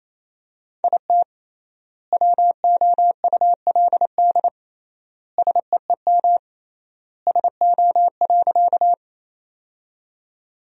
Morse code, heard as a keyed tone, about 28 wpm, 700 Hz. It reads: IT WOULD SEEM SO.
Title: 3x Word-Spacing - Sentences from Top 300 Words - 28wpm